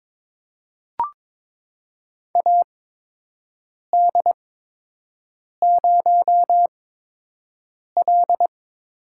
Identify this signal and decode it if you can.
Morse code, heard as a keyed tone, about 22 wpm, spaced as for 6 wpm, 700 Hz. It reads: AD0L
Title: USA Call Signs - 4 Characters - 8x Character Spacing - 22wpm